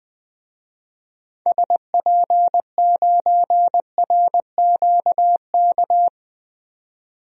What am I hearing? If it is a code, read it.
Morse code, heard as a keyed tone, about 20 wpm, 700 Hz. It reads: SP9RQK
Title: DXCC-100 Call Signs - 6 Characters - 20wpm